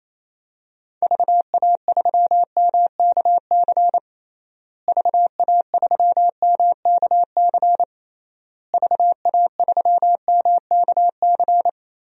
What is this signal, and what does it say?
Morse code, heard as a keyed tone, about 27 wpm, 700 Hz. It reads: VA3MKC VA3MKC VA3MKC
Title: DXCC-100 Call Signs - 6 Characters - Repeated 3x - 28wpm